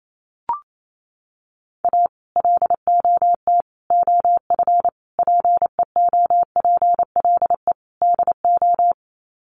Morse code, heard as a keyed tone, about 28 wpm, 700 Hz. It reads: A LOT OF PEOPLE DO
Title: Sets of 5 Words 28wpm